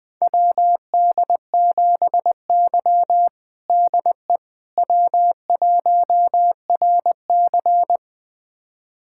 Morse code, heard as a keyed tone, about 21 wpm, 700 Hz. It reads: WD7Y DE W1RC
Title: USA Callsign DE Callsign - 4 Characters - 20wpm